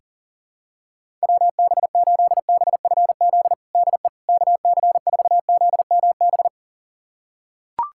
Morse code, heard as a keyed tone, about 40 wpm, 700 Hz. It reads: WB8BFZ DE KC4ZMB
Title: USA Callsign DE Callsign - 6 Characters - 40wpm